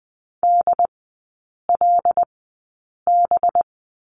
Morse code, heard as a keyed tone, about 20 wpm, spaced as for 8 wpm, 700 Hz. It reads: DLB
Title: 212 Lesson 34e - Number 8 - Three Characters 20wpm